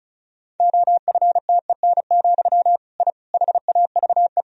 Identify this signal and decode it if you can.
Morse code, heard as a keyed tone, about 34 wpm, 700 Hz. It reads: OFTEN, I HAVE
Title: Sentences from Top 1000 Words - Review Part 1 - 35wpm